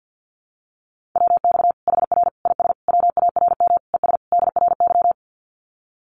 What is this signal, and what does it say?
Morse code, heard as a keyed tone, about 50 wpm, 700 Hz. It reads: WX HR IS WARM ES DRY
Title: Common QSO Phrases 50wpm